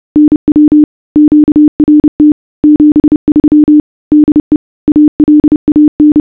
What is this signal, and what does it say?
Morse code, heard as a keyed tone, about 30 wpm, 300 Hz. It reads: NW QRT 73 DE ALAN